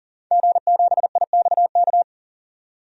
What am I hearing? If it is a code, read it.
Morse code, heard as a keyed tone, about 40 wpm, 700 Hz. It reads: G7IXK
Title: DXCC-100 Call Signs - 5 Characters - 40wpm